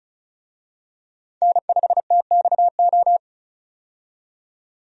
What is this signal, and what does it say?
Morse code, heard as a keyed tone, about 36 wpm, 700 Hz. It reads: N5TXO
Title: USA Call Signs - 5 Characters - 35wpm